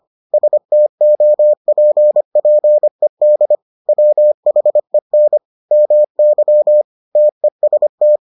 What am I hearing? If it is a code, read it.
Morse code, heard as a keyed tone, about 25 wpm, 600 Hz. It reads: STOPPED WHEN MY TEST